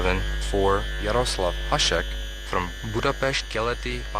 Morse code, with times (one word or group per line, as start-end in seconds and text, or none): none